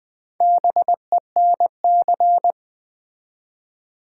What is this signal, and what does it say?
Morse code, heard as a keyed tone, about 20 wpm, 700 Hz. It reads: BENC
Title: Build A Word - 1 Letter At A Time - 5-Letter Words - 20wpm